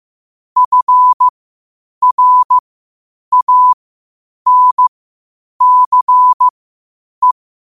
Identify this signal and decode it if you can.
Morse code, heard as a keyed tone, about 15 wpm, spaced as for 8 wpm, 1000 Hz. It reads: FRANCE
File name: qotdfiles / 8wpm